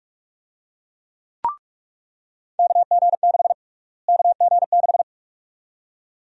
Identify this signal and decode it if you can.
Morse code, heard as a keyed tone, about 46 wpm, 700 Hz. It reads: KG6 KG6
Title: USA Call Sign Prefixes - Repeated 2x - 45wpm